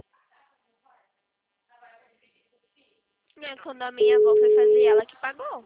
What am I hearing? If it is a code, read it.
Ringback tone (Europe, 425 Hz), with speech.